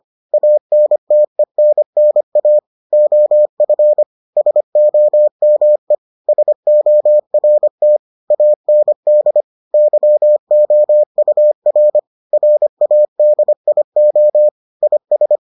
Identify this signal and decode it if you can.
Morse code, heard as a keyed tone, about 25 wpm, 600 Hz. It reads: ANTENNA OF SOME SORT AND YOUR RADIO IS